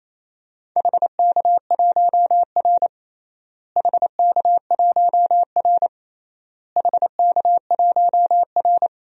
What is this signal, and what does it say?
Morse code, heard as a keyed tone, about 28 wpm, 700 Hz. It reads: HK1R HK1R HK1R
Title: DXCC-100 Call Signs - 4 Characters - Repeated 3x - 28wpm